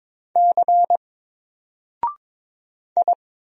Morse code, heard as a keyed tone, about 22 wpm, 700 Hz. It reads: C I
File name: Single-Letter-Number-22wpm